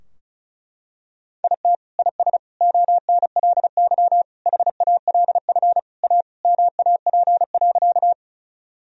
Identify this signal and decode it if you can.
Morse code, heard as a keyed tone, about 36 wpm, 700 Hz. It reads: IT IS ONLY HALF A MAP.
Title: Sentences from Top 1000 Words - Review Part 2 - 35wpm